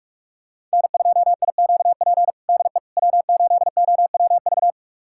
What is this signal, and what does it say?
Morse code, heard as a keyed tone, about 46 wpm, 700 Hz. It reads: N2IQP DE W8OWV
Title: USA Callsign DE Callsign - 5 Characters - 45wpm